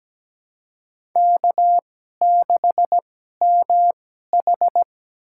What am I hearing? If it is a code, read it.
Morse code, heard as a keyed tone, about 17 wpm, spaced as for 12 wpm, 700 Hz. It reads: K6MH
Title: USA Call Signs - 4 Characters - 2x Character Spacing - 17wpm